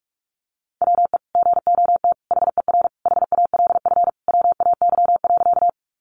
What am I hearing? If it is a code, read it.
Morse code, heard as a keyed tone, about 44 wpm, 700 Hz. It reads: WE GOT HER HALF WAY.
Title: Sentences from Top 400 Words 45wpm